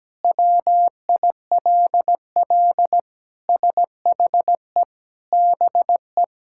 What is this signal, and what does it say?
Morse code, heard as a keyed tone, about 17 wpm, 700 Hz. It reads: WILL SHE BE
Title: Sentences from Top 1000 Words - Review Part 6 - 17wpm